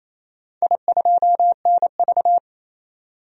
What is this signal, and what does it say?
Morse code, heard as a keyed tone, about 28 wpm, 700 Hz. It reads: I2NV
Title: DXCC-100 Call Signs - 4 Characters - 28wpm